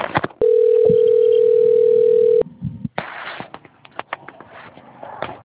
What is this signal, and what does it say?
Ringback tone (North America), 440 + 480 Hz.
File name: rg-1600-unknown- - - .38587